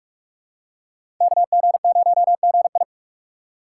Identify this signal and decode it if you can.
Morse code, heard as a keyed tone, about 41 wpm, 700 Hz. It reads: KG0GI
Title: USA Call Signs - 5 Characters - 3x Character Spacing - 45wpm